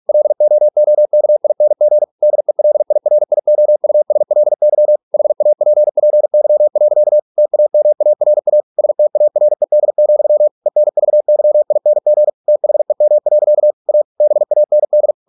Morse code, heard as a keyed tone, about 46 wpm, 600 Hz. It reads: LOOKING DELIRIOUSLY HAPPY. TAMARA STARED, ENVYING THEM. A BAND